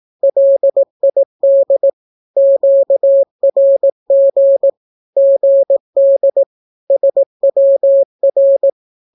Morse code, heard as a keyed tone, about 18 wpm, 550 Hz. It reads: LID QRG GD SWR
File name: naukapodstaw / skroty / skroty